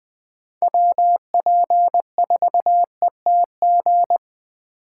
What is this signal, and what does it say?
Morse code, heard as a keyed tone, about 20 wpm, 700 Hz. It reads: WP4ETG